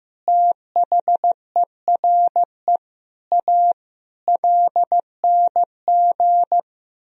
Morse code, heard as a keyed tone, about 15 wpm, 700 Hz. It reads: THERE A LNG
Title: Sentences from Top 400 Words and Abbreviations 15wpm